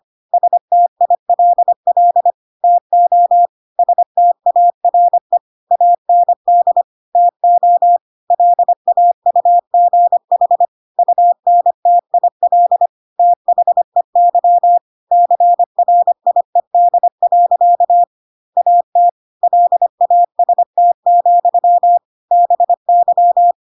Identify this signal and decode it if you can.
Morse code, heard as a keyed tone, about 25 wpm, 700 Hz. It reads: STILL TO STARE AND TO LAUGH UNTIL THEY CRIED. AT LAST, BY
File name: Pinocchio / Pinocchio ch3 25 25 ik1hge